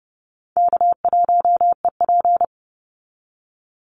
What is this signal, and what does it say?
Morse code, heard as a keyed tone, about 32 wpm, 700 Hz. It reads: K1EP
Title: Call Signs - 30wpm